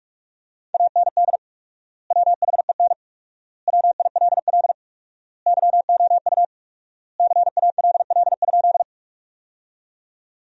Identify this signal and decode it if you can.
Morse code, heard as a keyed tone, about 46 wpm, 700 Hz. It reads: AND WHEN WILL YOU CALL?
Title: 4x Word-Spacing - Sentences from Top 100 Words - 45wpm